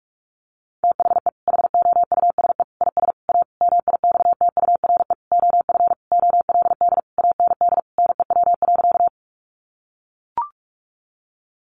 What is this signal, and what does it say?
Morse code, heard as a keyed tone, about 44 wpm, 700 Hz. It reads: THE HOUSE IS A MIXTURE OF OLD AND NEW.
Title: Sentences from Top 1500 Words 45wpm